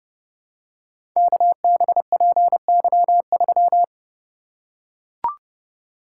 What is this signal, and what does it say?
Morse code, heard as a keyed tone, about 31 wpm, 700 Hz. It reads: KBPY3
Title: Code Groups - 5 Letter:Numbers - 30wpm